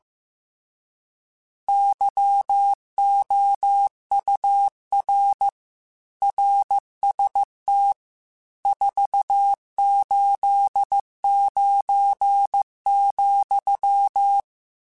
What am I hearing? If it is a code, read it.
Morse code, heard as a keyed tone, about 15 wpm, 800 Hz. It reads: YOUR RST 489,